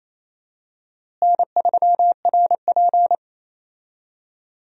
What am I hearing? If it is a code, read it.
Morse code, heard as a keyed tone, about 28 wpm, 700 Hz. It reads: N3RP
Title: USA Call Signs - 4 Characters - Repeated 2x - 28wpm